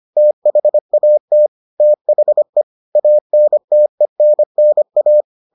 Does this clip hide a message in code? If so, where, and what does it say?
Yes: Morse code from 0.2 to 5.2 s, heard as a keyed tone, about 25 wpm, 600 Hz. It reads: THAT THE ANTENNA